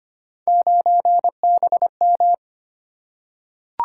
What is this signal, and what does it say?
Morse code, heard as a keyed tone, about 25 wpm, 700 Hz. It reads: 9BM
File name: Three Letter-Number Combinations - 25wpm